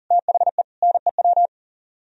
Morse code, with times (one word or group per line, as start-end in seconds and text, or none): 0.1-0.6 THE
0.8-1.5 NEW